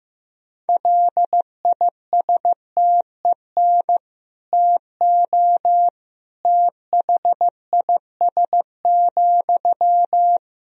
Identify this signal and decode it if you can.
Morse code, heard as a keyed tone, about 15 wpm, 700 Hz. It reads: LISTEN TO THIS,
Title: Sentences from Top 400 Words and Abbreviations 15wpm